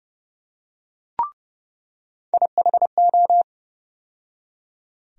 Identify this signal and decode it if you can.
Morse code, heard as a keyed tone, about 32 wpm, 700 Hz. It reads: IHO
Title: USA Call Sign Postfixes - 3 Letters 30wpm